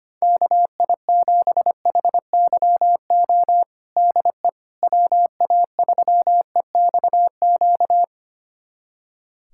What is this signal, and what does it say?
Morse code, heard as a keyed tone, about 25 wpm, 700 Hz. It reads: KI7HYO DE WA3EXQ
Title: USA Callsign DE Callsign - 6 Characters - 25wpm